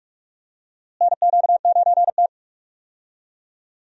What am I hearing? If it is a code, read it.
Morse code, heard as a keyed tone, about 41 wpm, 700 Hz. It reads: NQ9T
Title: USA Call Signs - 4 Characters - Repeated 2x - 45wpm